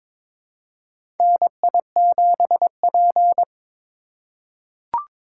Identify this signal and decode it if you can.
Morse code, heard as a keyed tone, about 22 wpm, 700 Hz. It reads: NI7P
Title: USA Call Signs - 2x Character Spacing - 22wpm